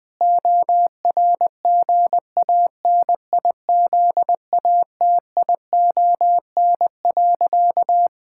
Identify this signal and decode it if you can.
Morse code, heard as a keyed tone, about 20 wpm, 700 Hz. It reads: ORGANIZATION.
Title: Sentences from Top 1000 Words - Review Part 5 - 20wpm